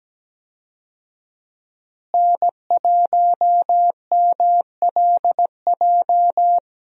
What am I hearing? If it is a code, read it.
Morse code, heard as a keyed tone, about 17 wpm, 700 Hz. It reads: N1MLJ
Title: USA Call Signs - 5 Characters - Repeated 3x - 17wpm